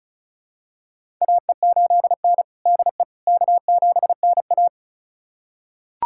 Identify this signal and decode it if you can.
Morse code, heard as a keyed tone, about 36 wpm, 700 Hz. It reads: AE8N DE K7NA
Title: USA Callsign DE Callsign - 4 Characters - 35wpm